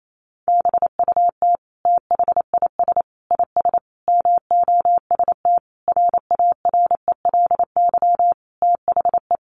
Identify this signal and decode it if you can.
Morse code, heard as a keyed tone, about 28 wpm, 700 Hz. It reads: BUT THIS IS MOST RARELY THE
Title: Sentences from Top 1400 Words 28wpm